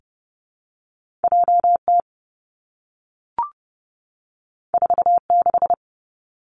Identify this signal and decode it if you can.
Morse code, heard as a keyed tone, about 30 wpm, 700 Hz. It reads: J T 4 6
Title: Two Challenging Letter-Number Combinations 30wpm